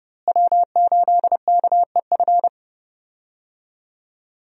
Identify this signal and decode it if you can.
Morse code, heard as a keyed tone, about 30 wpm, 700 Hz. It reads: W8KEF
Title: USA Call Signs - 5 Characters - Repeated 2x - 30wpm